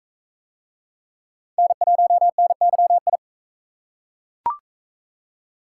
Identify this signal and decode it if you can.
Morse code, heard as a keyed tone, about 41 wpm, 700 Hz. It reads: N1NYI